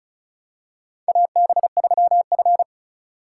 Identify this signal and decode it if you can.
Morse code, heard as a keyed tone, about 36 wpm, 700 Hz. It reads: AB3F
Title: USA Call Signs - 4 Characters - 2x Character Spacing - 35wpm